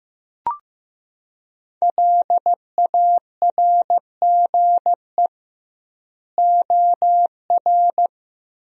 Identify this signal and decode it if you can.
Morse code, heard as a keyed tone, about 16 wpm, 700 Hz. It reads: LARGE OR
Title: 2x Word-Spacing - Sentences from Top 200 Words - 15wpm